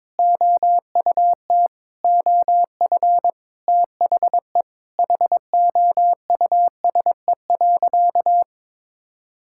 Morse code, heard as a keyed tone, about 22 wpm, 700 Hz. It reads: OUT OF THE HOUSE.
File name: practice / 22wpm / Sentences with 4-Word Phrases - 22wpm